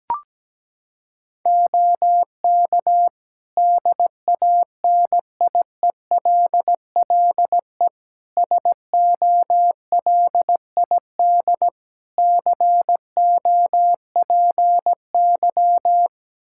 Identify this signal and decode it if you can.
Morse code, heard as a keyed tone, about 17 wpm, 700 Hz. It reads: OK DANIELLE SOLID COPY
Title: Practice QSO - Exchange 3 - More Info - 17wpm